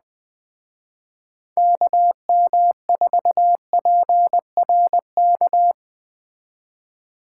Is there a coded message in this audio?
Yes: Morse code from 1.6 to 5.7 s, heard as a keyed tone, about 21 wpm, 700 Hz. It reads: KM4PRK